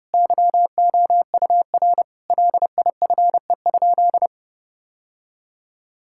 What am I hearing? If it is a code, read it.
Morse code, heard as a keyed tone, about 28 wpm, 700 Hz. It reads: YOUR LIFE?
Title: Sentences from Top 1000 Words - Review Part 6 - 30wpm